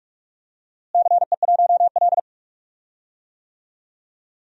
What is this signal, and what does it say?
Morse code, heard as a keyed tone, about 46 wpm, 700 Hz. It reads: CE1L
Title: DXCC-100 Call Signs - 4 Characters - Repeated 2x - 45wpm